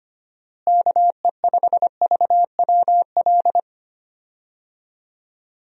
Morse code, heard as a keyed tone, about 25 wpm, 700 Hz. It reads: KE5VWL
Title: USA Call Signs - 6 Characters - Repeated 2x - 25wpm